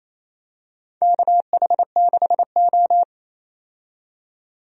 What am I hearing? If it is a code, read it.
Morse code, heard as a keyed tone, about 28 wpm, 690 Hz. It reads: KH6O
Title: DXCC-100 Call Signs - 4 Characters - 28wpm